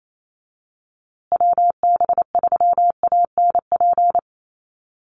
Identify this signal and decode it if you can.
Morse code, heard as a keyed tone, about 28 wpm, 700 Hz. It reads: WB3ANP
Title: USA Call Signs Repeated 2x 28wpm